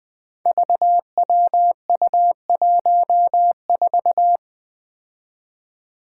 Morse code, heard as a keyed tone, about 20 wpm, 700 Hz. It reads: VWU14
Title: Five Challenging Letter-Number Combinations 20wpm